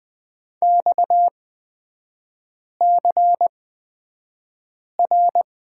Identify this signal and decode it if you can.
Morse code, heard as a keyed tone, about 20 wpm, spaced as for 5 wpm, 700 Hz. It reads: XCR
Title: 236 Lesson 38e - Letter q - Three Characters 20wpm